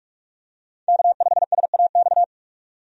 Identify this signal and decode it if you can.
Morse code, heard as a keyed tone, about 46 wpm, 700 Hz. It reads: K5SAX
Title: USA Call Signs - 5 Characters - 45wpm